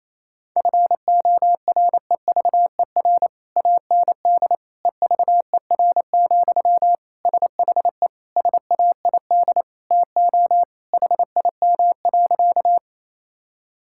Morse code, heard as a keyed tone, about 28 wpm, 700 Hz. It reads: FOREVER AND EVER, SHE SAID TO HIM.
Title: Sentences from Top 1600 Words 28wpm